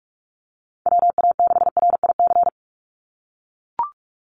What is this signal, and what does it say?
Morse code, heard as a keyed tone, about 44 wpm, 690 Hz. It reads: WA6RIC
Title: NA SOTA Chaser Callsigns 45wpm